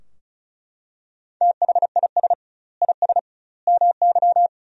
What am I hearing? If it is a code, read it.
Morse code, heard as a keyed tone, about 34 wpm, 700 Hz. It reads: THIS IS MY
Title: 2x Word-Spacing - Sentences from Top 300 Words - 35wpm